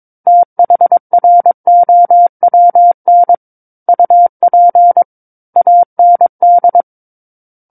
Morse code, heard as a keyed tone, about 22 wpm, 700 Hz. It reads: THROWN UP AND
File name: Book One / 13-Chapter 13 - 22 WPM